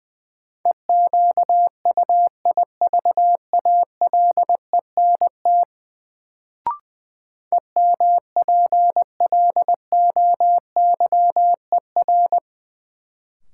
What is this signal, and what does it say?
Morse code, heard as a keyed tone, about 20 wpm, 700 Hz. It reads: EQUIVALENT EMPLOYER
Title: Sentences from Top 1600 Words - New Word Drill - 20wpm